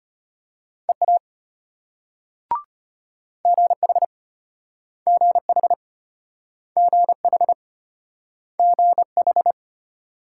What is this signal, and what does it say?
Morse code, heard as a keyed tone, about 34 wpm, 700 Hz. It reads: EA GH GH GH GH